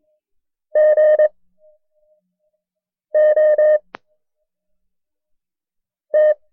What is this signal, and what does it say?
Morse code, heard as a keyed tone, about 22 wpm, spaced as for 4 wpm, 610 Hz. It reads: GOT
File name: FreeFiles / CW-practice / KB6MT-tape1-side2